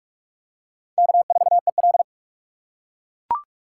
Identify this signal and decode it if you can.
Morse code, heard as a keyed tone, about 40 wpm, 700 Hz. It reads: K4EL